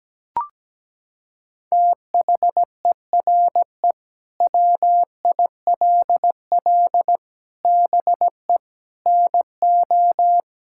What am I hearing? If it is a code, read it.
Morse code, heard as a keyed tone, about 17 wpm, 700 Hz. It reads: THERE WILL BE NO
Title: Sentences from Top 1000 Words - Review Part 5 - 17wpm